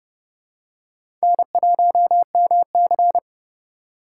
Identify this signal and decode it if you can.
Morse code, heard as a keyed tone, about 31 wpm, 700 Hz. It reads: N1MC
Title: USA Call Signs - 4x Character Spacing - 30wpm